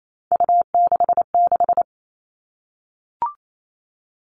Morse code, heard as a keyed tone, about 28 wpm, 700 Hz. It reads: U66
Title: All Cut Numbers - 3-Digit 28wpm